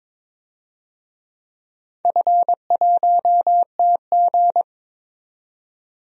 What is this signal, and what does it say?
Morse code, heard as a keyed tone, about 22 wpm, 700 Hz. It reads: F1TG
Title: DXCC-100 Call Signs - 4 Characters - 22wpm